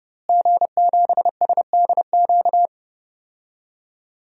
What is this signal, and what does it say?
Morse code, heard as a keyed tone, about 30 wpm, 700 Hz. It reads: G7SDQ
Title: DXCC-100 Call Signs - 5 Characters - Repeated 3x - 30wpm